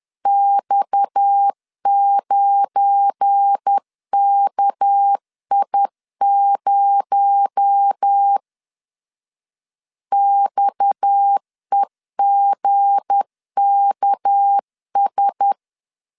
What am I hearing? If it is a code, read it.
Morse code, heard as a keyed tone, about 10 wpm, 800 Hz. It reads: X9KI0 XEGKS
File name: LEZIONI / 7 GF9Z / 7 GF9 10wpm 2747